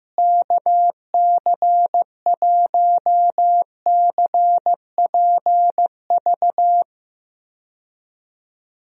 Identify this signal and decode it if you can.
Morse code, heard as a keyed tone, about 15 wpm, 700 Hz. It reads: KC1CPV
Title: USA Call Signs - 6 Characters - Repeated 3x - 15wpm